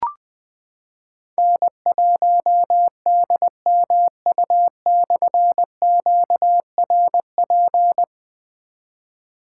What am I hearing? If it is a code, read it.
Morse code, heard as a keyed tone, about 20 wpm, 700 Hz. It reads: N1DMU/QRP
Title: Difficult Call Signs 20wpm